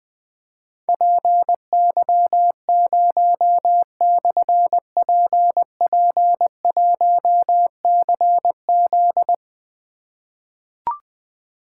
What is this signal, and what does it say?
Morse code, heard as a keyed tone, about 20 wpm, 700 Hz. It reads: PY0/PP1CZ
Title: Difficult Call Signs 20wpm